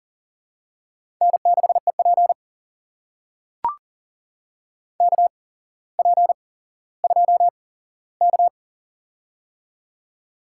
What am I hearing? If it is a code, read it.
Morse code, heard as a keyed tone, about 40 wpm, spaced as for 17 wpm, 700 Hz. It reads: N6EP KP2K
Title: USA Call Signs - 4 Characters - 8x Character Spacing - 40wpm